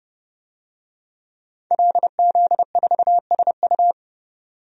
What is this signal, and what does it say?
Morse code, heard as a keyed tone, about 30 wpm, 700 Hz. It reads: LZ4SU
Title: DXCC-100 Call Signs - 5 Characters - 30wpm